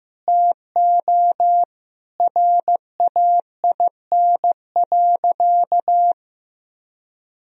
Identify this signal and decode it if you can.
Morse code, heard as a keyed tone, about 15 wpm, 700 Hz. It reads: TO RAIN.
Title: Sentences from Top 1000 Words - Review Part 1 - 15wpm